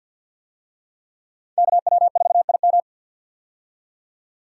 Morse code, heard as a keyed tone, about 50 wpm, 700 Hz. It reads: KW4IM
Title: USA Call Signs - 5 Characters - 50wpm